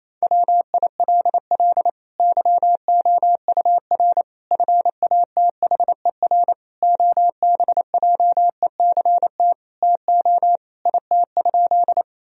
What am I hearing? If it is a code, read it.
Morse code, heard as a keyed tone, about 28 wpm, 700 Hz. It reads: WILL YOUR FATHER OBJECT TO IT?
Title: Sentences from Top 1000 Words - Review Part 3 - 28wpm